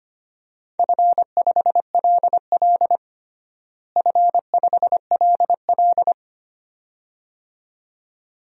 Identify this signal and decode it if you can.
Morse code, heard as a keyed tone, about 25 wpm, 700 Hz. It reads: F5LL F5LL
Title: DXCC-100 Call Signs - 4 Characters - Repeated 2x - 25wpm